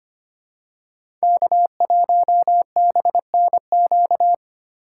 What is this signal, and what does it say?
Morse code, heard as a keyed tone, about 25 wpm, 700 Hz. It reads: K1BNQ
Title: USA Call Signs - 3x Character Spacing - 25wpm